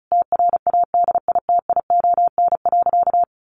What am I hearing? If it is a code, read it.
Morse code, heard as a keyed tone, about 34 wpm, 700 Hz. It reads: TRADITION.
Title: Sentences from Top 1500 Words 35wpm